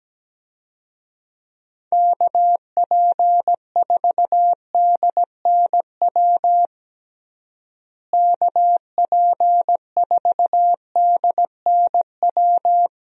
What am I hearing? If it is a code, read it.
Morse code, heard as a keyed tone, about 17 wpm, 700 Hz. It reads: KP4DNW KP4DNW
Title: USA Call Signs - 6 Characters - Repeated 3x - 17wpm